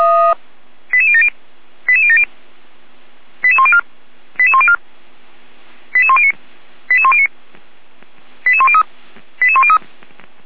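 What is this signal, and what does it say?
Five-tone selective call (ZVEI), heard as quick runs of five tones, tones of about 70 ms. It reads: 88070 88070 88104 88104 88109 88109 88103 88103
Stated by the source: ZVEI five-tone selective call received on an SDR.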